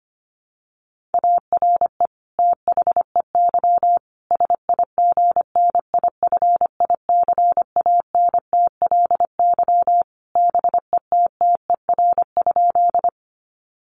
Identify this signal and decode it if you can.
Morse code, heard as a keyed tone, about 25 wpm, 700 Hz. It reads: ARE THEY SIGNIFICANTLY BETTER?
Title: Sentences from Top 1500 Words 25wpm